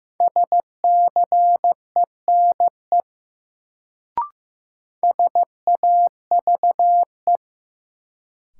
Morse code, heard as a keyed tone, about 15 wpm, 700 Hz. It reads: SCENE SAVE
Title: Sentences from Top 1100 Words - New Word Drill Alt - 15wpm